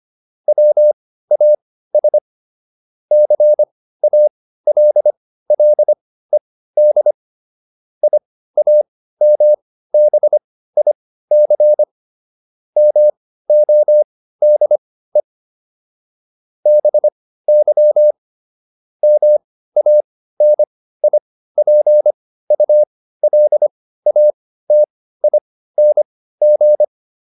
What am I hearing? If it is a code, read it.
Morse code, heard as a keyed tone, about 25 wpm, spaced as for 15 wpm, 600 Hz. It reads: WAS CALLED IAMBIC MODE BY MANIPULATING